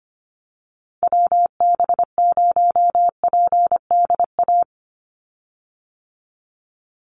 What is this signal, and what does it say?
Morse code, heard as a keyed tone, about 25 wpm, 700 Hz. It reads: WB0PDA
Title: USA Call Signs 25wpm